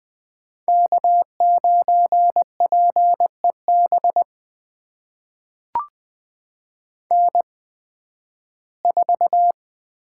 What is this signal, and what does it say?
Morse code, heard as a keyed tone, about 20 wpm, spaced as for 13 wpm, 700 Hz. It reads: K9PEB N4
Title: USA Call Signs - 8x Character Spacing - 20wpm